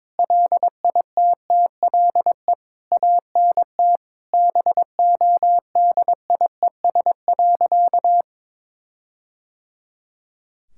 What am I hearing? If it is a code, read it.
Morse code, heard as a keyed tone, about 22 wpm, 700 Hz. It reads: LITTLE ANT BODIES.
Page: morse-code-ninja-practice-22wpm